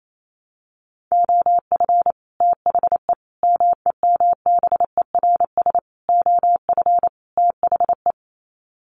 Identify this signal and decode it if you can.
Morse code, heard as a keyed tone, about 28 wpm, 700 Hz. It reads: OF THE MEMBERS OF THE
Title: Sets of 5 Words Encore2 28wpm